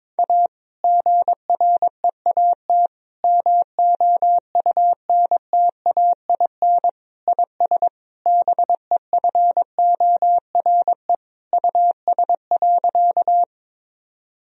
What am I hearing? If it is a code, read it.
Morse code, heard as a keyed tone, about 21 wpm, 700 Hz. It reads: A GREAT MOUNTAIN IS BEFORE US.
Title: Sentences from Top 1000 Words - Review Part 1 - 22wpm